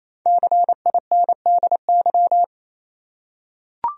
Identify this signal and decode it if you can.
Morse code, heard as a keyed tone, about 28 wpm, 700 Hz. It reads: CINDY
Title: USA Common Names - Max 5 Letters - 28wpm